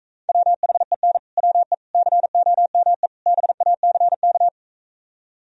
Morse code, heard as a keyed tone, about 45 wpm, 700 Hz. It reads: WHEN WE COME BACK